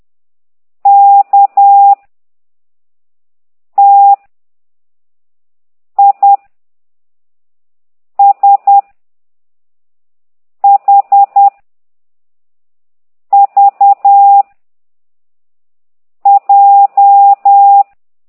Morse code, heard as a keyed tone, about 10 wpm, 800 Hz. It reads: K T I S H V J
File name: exerc7